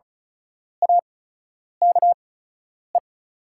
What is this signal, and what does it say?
Morse code, heard as a keyed tone, about 36 wpm, spaced as for 10 wpm, 690 Hz. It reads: AKE
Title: 8x Character Spacing - Top 500 Words - 35wpm